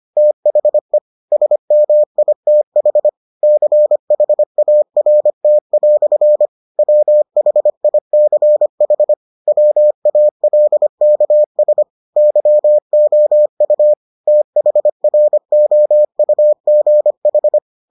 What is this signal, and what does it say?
Morse code, heard as a keyed tone, about 25 wpm, 600 Hz. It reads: THE SMITH CHART" WHICH WALKS YOU THROUGH